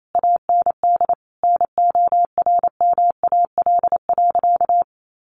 Morse code, heard as a keyed tone, about 28 wpm, 700 Hz. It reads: AND NORMAL.
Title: Sentences from Top 1200 Words 28wpm